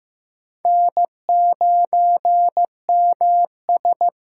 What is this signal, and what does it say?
Morse code, heard as a keyed tone, about 15 wpm, 700 Hz. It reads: N9MS
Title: K1USN SST QSO Practice - 15wpm